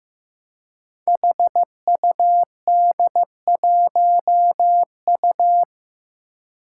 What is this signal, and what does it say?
Morse code, heard as a keyed tone, about 15 wpm, 700 Hz. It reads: HUD1U
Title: Five Challenging Letter-Number Combinations 15wpm